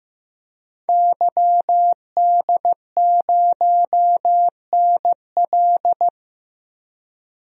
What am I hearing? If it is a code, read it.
Morse code, heard as a keyed tone, about 15 wpm, 700 Hz. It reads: YD0NL
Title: DXCC-100 Call Signs - 5 Characters - Repeated 2x - 15wpm